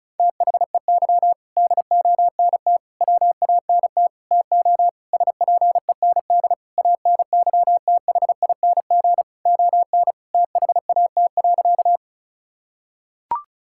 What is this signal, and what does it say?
Morse code, heard as a keyed tone, about 36 wpm, 700 Hz. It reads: THEY DONT WANT TO SPEND ANYTHING ON THAT.
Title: Sentences from Top 1000 Words - Review Part 5 - 35wpm